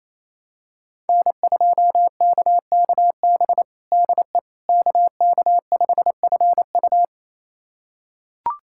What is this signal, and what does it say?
Morse code, heard as a keyed tone, about 28 wpm, 700 Hz. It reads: N2KKB DE KK5FU
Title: USA Callsign DE Callsign 28wpm